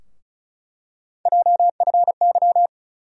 Morse code, heard as a keyed tone, about 34 wpm, 700 Hz. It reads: JFY
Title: USA Call Sign Postfixes - 3 Letters 35wpm